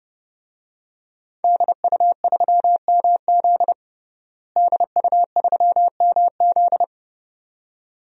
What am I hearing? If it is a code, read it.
Morse code, heard as a keyed tone, about 30 wpm, 700 Hz. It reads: DU3MZ DU3MZ
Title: DXCC-100 Call Signs - 5 Characters - Repeated 2x - 30wpm